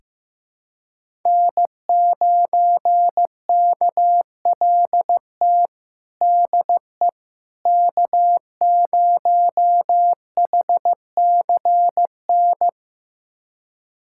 Morse code, heard as a keyed tone, about 16 wpm, 700 Hz. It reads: N9KLT DE K0HCN
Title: USA Callsign DE Callsign - 5 Characters - 15wpm